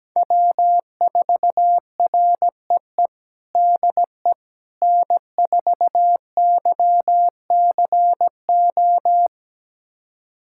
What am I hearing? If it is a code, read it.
Morse code, heard as a keyed tone, about 17 wpm, 700 Hz. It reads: W4REE DE N4YCO
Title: USA Callsign DE Callsign - 5 Characters - 17wpm